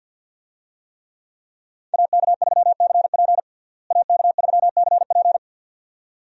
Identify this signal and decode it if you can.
Morse code, heard as a keyed tone, about 47 wpm, 700 Hz. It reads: AK3CP AK3CP
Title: USA Call Signs - 5 Characters - Repeated 2x - 50wpm